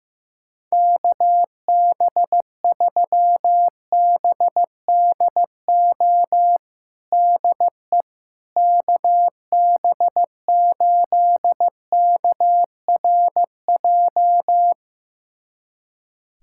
Morse code, heard as a keyed tone, about 15 wpm, 700 Hz. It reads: KB3BDO DE KB8KRJ
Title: USA Callsign DE Callsign 15wpm